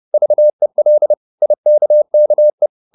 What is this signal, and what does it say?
Morse code, heard as a keyed tone, about 30 wpm, 600 Hz. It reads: VEL IKKE
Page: Ferie 30 wpm